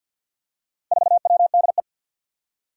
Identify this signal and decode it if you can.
Morse code, heard as a keyed tone, about 51 wpm, 700 Hz. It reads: 4WDE
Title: Four Challenging Letter-Number Combinations 50wpm